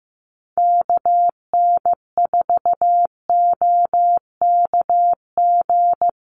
Morse code, heard as a keyed tone, about 15 wpm, 700 Hz. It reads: KN4OKG
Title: USA Call Signs Repeated 3x 15wpm